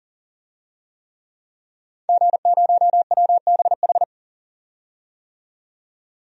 Morse code, heard as a keyed tone, about 40 wpm, 700 Hz. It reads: G0WBH
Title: DXCC-100 Call Signs - 5 Characters - 40wpm